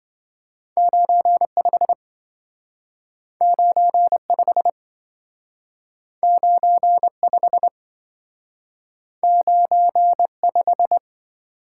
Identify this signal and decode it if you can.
Morse code, heard as a keyed tone, about 27 wpm, 700 Hz. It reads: 95 95 95 95